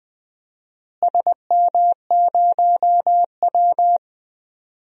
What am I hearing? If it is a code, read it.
Morse code, heard as a keyed tone, about 21 wpm, 700 Hz. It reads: SM0W
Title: DXCC-100 Call Signs - 4 Characters - Repeated 3x - 20wpm